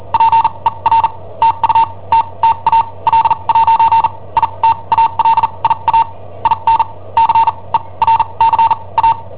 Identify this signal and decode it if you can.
Morse code, heard as a keyed tone, about 43 wpm, spaced as for 28 wpm, 910 Hz. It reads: PER TUTTA L'ITALIA IN CERCA
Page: Cassetta esercizi preparata da I7OHP per IW5ELP